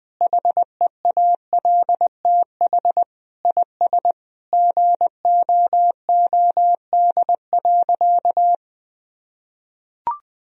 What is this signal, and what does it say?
Morse code, heard as a keyed tone, about 20 wpm, 700 Hz. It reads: HEALTH IS GOOD.
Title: Sentences from Top 1000 Words - Review Part 3 - 20wpm